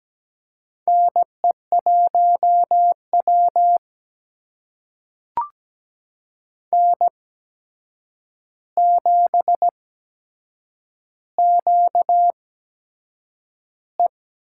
Morse code, heard as a keyed tone, about 17 wpm, spaced as for 7 wpm, 700 Hz. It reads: NE1W N7QE
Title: USA Call Signs - 4 Characters - 8x Character Spacing - 17wpm